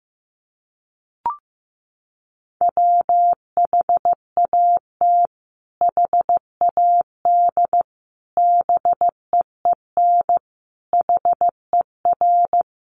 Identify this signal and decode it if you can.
Morse code, heard as a keyed tone, about 15 wpm, 700 Hz. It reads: WHAT HAD BEEN HER
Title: Sentences from Top 1500 Words 15wpm